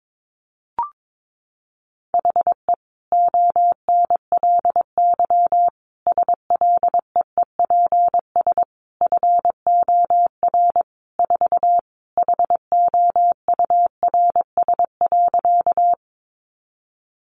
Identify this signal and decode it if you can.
Morse code, heard as a keyed tone, about 22 wpm, 700 Hz. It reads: HE ONLY SLEEPS FOR 4 HOURS.
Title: Additional Easy Phrases 22wpm